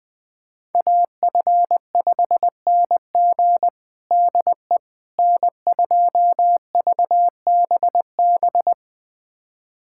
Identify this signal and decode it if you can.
Morse code, heard as a keyed tone, about 20 wpm, 700 Hz. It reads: AF5NG DE N2VBB
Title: USA Callsign DE Callsign - 5 Characters - 20wpm